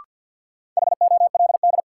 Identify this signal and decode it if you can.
Morse code, heard as a keyed tone, about 53 wpm, 700 Hz. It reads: HOLD
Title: Sentences from Top 400 Words - New Word Drill - 50wpm